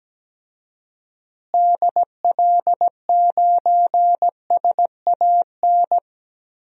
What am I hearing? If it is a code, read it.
Morse code, heard as a keyed tone, about 17 wpm, 700 Hz. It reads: DL9SAN
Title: DXCC-100 Call Signs - 6 Characters - Repeated 2x - 17wpm